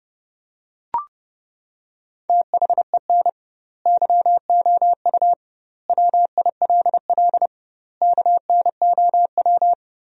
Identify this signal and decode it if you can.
Morse code, heard as a keyed tone, about 30 wpm, 700 Hz. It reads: THEN YOU WILL KNOW
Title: 2x Word-Spacing - Sentences from Top 400 Words - 30wpm